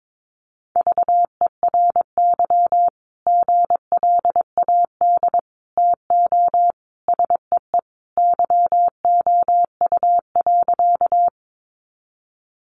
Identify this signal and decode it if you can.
Morse code, heard as a keyed tone, about 22 wpm, 700 Hz. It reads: VERY GLAD TO SEE YOU.
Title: Sentences from Top 800 Words 22wpm